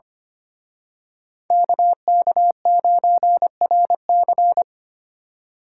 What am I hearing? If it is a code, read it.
Morse code, heard as a keyed tone, about 31 wpm, 700 Hz. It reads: KK9RC
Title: Speed Racing - USA 2x2 Call Signs 25wpm